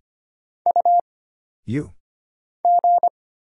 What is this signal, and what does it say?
Morse code, heard as a keyed tone, about 25 wpm, 700 Hz. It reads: U G
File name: Single Character (Letter:Number:Punctuation:Prosign) - Mind-Melt - 25wpm